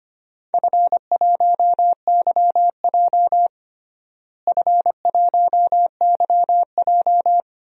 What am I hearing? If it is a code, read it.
Morse code, heard as a keyed tone, about 25 wpm, 700 Hz. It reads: F1YJ F1YJ
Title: DXCC-100 Call Signs - 4 Characters - Repeated 2x - 25wpm